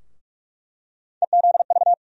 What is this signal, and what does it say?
Morse code, heard as a keyed tone, about 41 wpm, 700 Hz. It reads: EZV
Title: USA Call Sign Postfixes - 3 Letters 45wpm